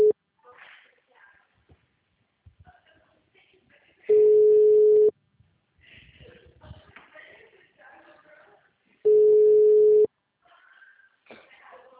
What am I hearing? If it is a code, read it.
Ringback tone (Europe, 425 Hz).
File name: out- -2005- - - .45547